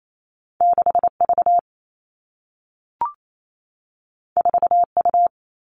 Morse code, heard as a keyed tone, about 28 wpm, 700 Hz. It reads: 6 V 4 U
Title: Two Challenging Letter-Number Combinations 28wpm